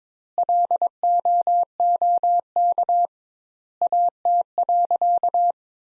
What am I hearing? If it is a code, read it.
Morse code, heard as a keyed tone, about 21 wpm, 700 Hz. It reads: LOOK AT.
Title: Sentences Top 200 Words - 22wpm